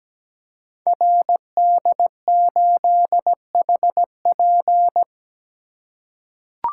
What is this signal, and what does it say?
Morse code, heard as a keyed tone, about 17 wpm, 700 Hz. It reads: RD8HP
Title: Code Groups - 5 Letter:Numbers - 17wpm